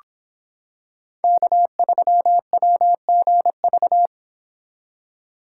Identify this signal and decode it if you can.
Morse code, heard as a keyed tone, about 26 wpm, 700 Hz. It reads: K3WGV